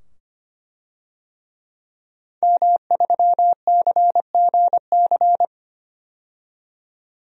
Morse code, heard as a keyed tone, about 25 wpm, 700 Hz. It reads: M3CGC
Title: DXCC-100 Call Signs - 5 Characters - 25wpm